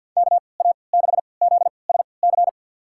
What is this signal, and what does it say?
Morse code, heard as a keyed tone, about 52 wpm, spaced as for 29 wpm, 700 Hz. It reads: KA6ZSC
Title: USA Call Signs - 6 Characters - 3x Character Spacing - 50wpm